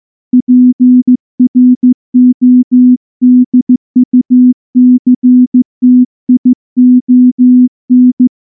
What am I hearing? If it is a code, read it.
Morse code, heard as a keyed tone, about 14 wpm, 250 Hz. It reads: PRODUCTION
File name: morse-training-15WPM